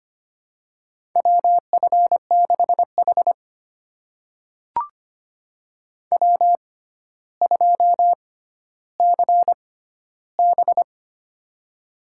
Morse code, heard as a keyed tone, about 25 wpm, spaced as for 13 wpm, 700 Hz. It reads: WF6H W2CB
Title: USA Call Signs - 6x Character Spacing - 25wpm